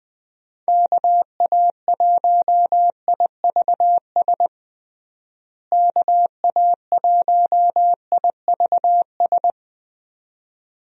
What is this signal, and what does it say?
Morse code, heard as a keyed tone, about 20 wpm, 700 Hz. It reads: KA1IVS KA1IVS
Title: USA Call Signs - 6 Characters - Repeated 2x - 20wpm